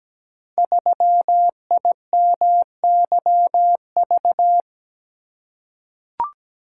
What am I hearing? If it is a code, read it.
Morse code, heard as a keyed tone, about 17 wpm, 700 Hz. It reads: 3IMYV
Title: Code Groups - 5 Letter:Numbers - 17wpm